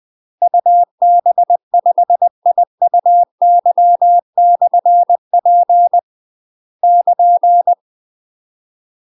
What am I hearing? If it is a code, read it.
Morse code, heard as a keyed tone, about 20 wpm, 700 Hz. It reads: UB5IUY/P <KN>
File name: UK MASTER QSOs TRAINING / 20WPM / UK MASTER QSOs 0070